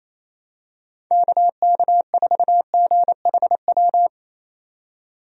Morse code, heard as a keyed tone, about 28 wpm, 700 Hz. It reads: KK4GHW
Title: USA Call Signs - 6 Characters - 28wpm